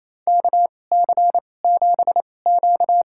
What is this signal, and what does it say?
Morse code, heard as a keyed tone, about 28 wpm, spaced as for 20 wpm, 690 Hz. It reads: KC7Q
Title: USA Call Signs - 2x Character Spacing - 28wpm